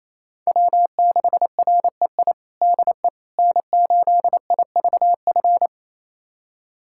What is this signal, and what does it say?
Morse code, heard as a keyed tone, about 28 wpm, 700 Hz. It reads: W6REI DE N8IVF
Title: USA Callsign DE Callsign 28wpm